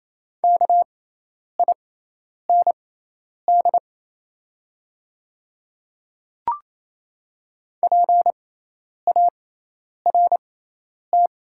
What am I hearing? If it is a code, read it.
Morse code, heard as a keyed tone, about 27 wpm, spaced as for 10 wpm, 700 Hz. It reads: KIND PART
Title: 6x Character Spacing - Top 200 Words - 28wpm